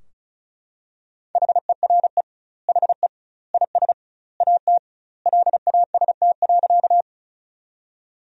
Morse code, heard as a keyed tone, about 36 wpm, 700 Hz. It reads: HERE HE IS AT LAST.
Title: 2x Word-Spacing - Sentences from Top 300 Words - 35wpm